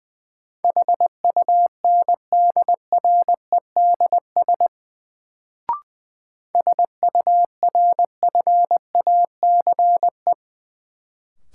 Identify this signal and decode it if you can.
Morse code, heard as a keyed tone, about 20 wpm, 700 Hz. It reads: HUNDREDS SURFACE
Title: Sets of 4 Words - New Word Drill - 20wpm